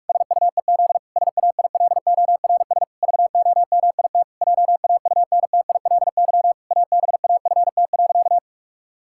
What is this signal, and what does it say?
Morse code, heard as a keyed tone, about 44 wpm, 700 Hz. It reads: SUEZ SAILORS VOMIT JAUNTILY ABAFT.
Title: Pangram Challenge v2 - 45wpm